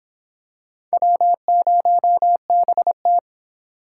Morse code, heard as a keyed tone, about 24 wpm, 700 Hz. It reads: W0BT